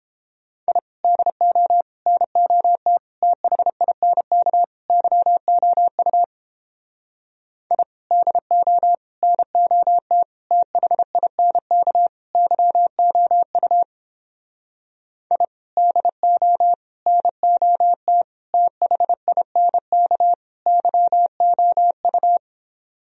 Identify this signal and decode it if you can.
Morse code, heard as a keyed tone, about 29 wpm, 700 Hz. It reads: I DO NOT THINK YOU I DO NOT THINK YOU I DO NOT THINK YOU